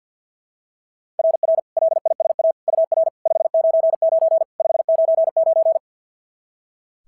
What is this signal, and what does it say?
Morse code, heard as a keyed tone, about 46 wpm, 660 Hz. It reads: RR LISA UR 599 599
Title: SOTA QSO Practice 50wpm